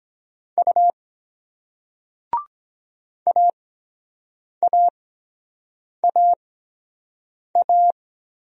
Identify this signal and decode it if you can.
Morse code, heard as a keyed tone, about 23 wpm, spaced as for 7 wpm, 700 Hz. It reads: U AAAA